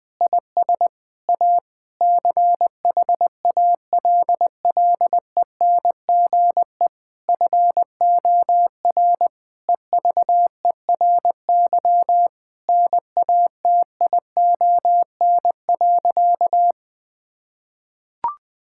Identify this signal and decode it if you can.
Morse code, heard as a keyed tone, about 20 wpm, 700 Hz. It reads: IS A CHALLENGE FOR EVERY NATION.
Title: Difficult Sentences with Hints - 20wpm